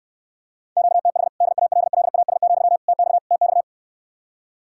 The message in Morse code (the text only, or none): QTH DALLAS, TX TX